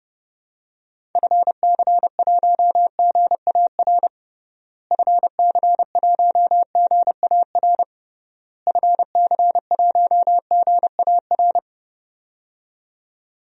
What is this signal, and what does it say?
Morse code, heard as a keyed tone, about 30 wpm, 700 Hz. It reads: FC1GAR FC1GAR FC1GAR
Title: DXCC-100 Call Signs - 6 Characters - Repeated 3x - 30wpm